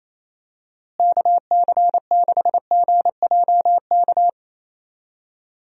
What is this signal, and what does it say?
Morse code, heard as a keyed tone, about 27 wpm, 700 Hz. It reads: KC6GJK
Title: USA Call Signs - 6 Characters - Repeated 2x - 28wpm